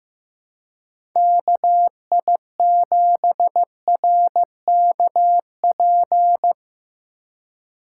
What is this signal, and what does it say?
Morse code, heard as a keyed tone, about 16 wpm, 700 Hz. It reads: KI7RKP
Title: Speed Racing - USA 2x3 Call Signs 15wpm